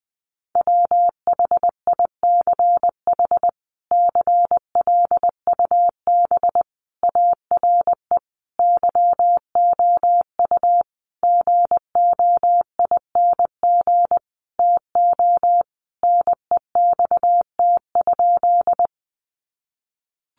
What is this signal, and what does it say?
Morse code, heard as a keyed tone, about 20 wpm, 700 Hz. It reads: WHICH CLUB ARE YOU GOING TO NEXT?
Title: Sentences from Top 1200 Words 20wpm